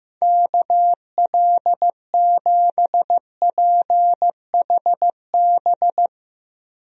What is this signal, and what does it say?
Morse code, heard as a keyed tone, about 15 wpm, 690 Hz. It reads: KL7PHB
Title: DXCC-100 Call Signs - 6 Characters - 15wpm